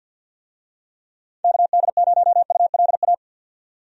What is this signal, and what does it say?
Morse code, heard as a keyed tone, about 51 wpm, 700 Hz. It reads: KD0ULA
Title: USA Call Signs - 6 Characters - 50wpm